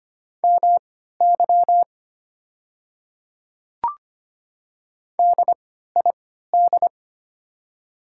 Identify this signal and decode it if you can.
Morse code, heard as a keyed tone, about 25 wpm, spaced as for 14 wpm, 700 Hz. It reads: MY DID
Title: 3x Character Spacing - Top 200 Words - 25wpm